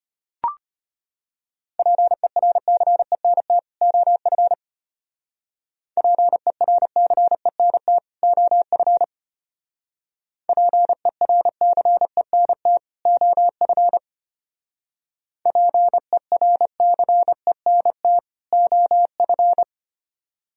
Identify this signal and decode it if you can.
Morse code, heard as a keyed tone, about 31 wpm, 700 Hz. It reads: PERCENT OF PERCENT OF PERCENT OF PERCENT OF
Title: Speed-Racing - Sets of 2 Words 25wpm